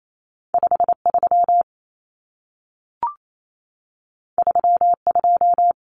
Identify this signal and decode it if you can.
Morse code, heard as a keyed tone, about 28 wpm, 700 Hz. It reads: 53 32
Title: Common Cut Numbers - 2-Digit 28wpm